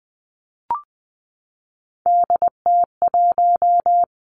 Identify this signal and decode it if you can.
Morse code, heard as a keyed tone, about 20 wpm, 700 Hz. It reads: DT1
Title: Three Challenging Letter-Number Combinations 20wpm